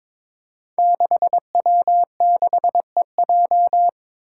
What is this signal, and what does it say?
Morse code, heard as a keyed tone, about 22 wpm, 700 Hz. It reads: 6W6EJ
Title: Five Challenging Letter-Number Combinations 22wpm